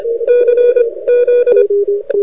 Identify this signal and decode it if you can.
Morse code, heard as a keyed tone, about 25 wpm, 2450 Hz, with radio noise. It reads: C Z
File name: rec contest / arrl cw 2018 / test 20